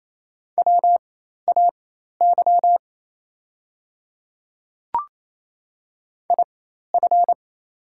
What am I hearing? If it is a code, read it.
Morse code, heard as a keyed tone, about 28 wpm, spaced as for 13 wpm, 700 Hz. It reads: WAY IF
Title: 4x Character Spacing - Top 100 Words - 28wpm